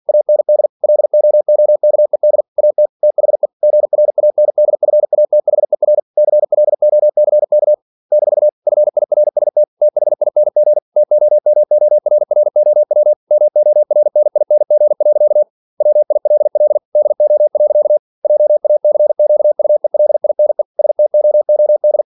Morse code, heard as a keyed tone, about 50 wpm, 600 Hz. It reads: AND LOOKED AT THE GRANDFATHER CLOCK - FIRST THING TOMORROW MORNING. WILL DO. JACQUELINE STOOD
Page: Never 47wpm